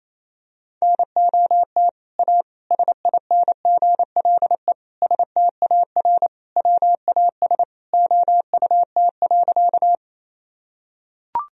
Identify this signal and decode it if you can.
Morse code, heard as a keyed tone, about 28 wpm, 700 Hz. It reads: NOT A SINGLE STAR WAS OUT.
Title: Sentences from Top 1000 Words - Review Part 4 - 28wpm